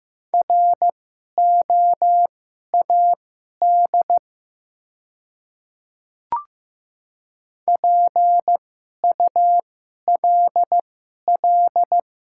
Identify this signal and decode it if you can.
Morse code, heard as a keyed tone, about 15 wpm, spaced as for 11 wpm, 700 Hz. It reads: ROAD PULL
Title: 2x Character Spacing - Top 500 Words - 15wpm